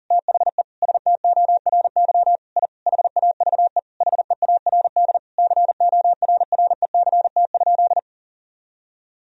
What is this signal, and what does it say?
Morse code, heard as a keyed tone, about 40 wpm, 700 Hz. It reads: THE STORY I HAVE HEARD CORRECT?
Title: Sentences from Top 1000 Words - Review Part 2 - 40wpm